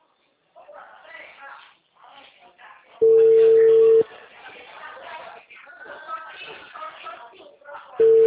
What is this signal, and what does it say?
Ringback tone (China, 450 Hz), with speech.
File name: out- -2019- - - .47151